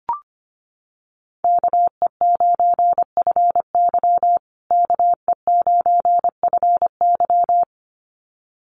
Morse code, heard as a keyed tone, about 25 wpm, 700 Hz. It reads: KE9FY KE9FY
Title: USA Call Signs Repeated 2x 25wpm